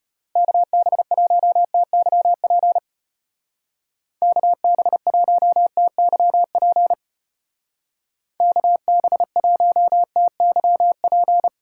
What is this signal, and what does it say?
Morse code, heard as a keyed tone, about 32 wpm, 700 Hz. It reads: KB1TYP KB1TYP KB1TYP